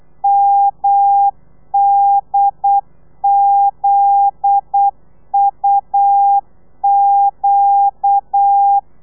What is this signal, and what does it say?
Morse code, heard as a keyed tone, about 8 wpm, 800 Hz. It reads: MDZUQ
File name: telegrafia / i2tfj / Lez9 35